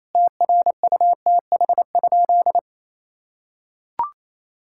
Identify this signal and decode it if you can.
Morse code, heard as a keyed tone, about 27 wpm, 700 Hz. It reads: TRUTH?
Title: Sentences from Top 1600 Words 28wpm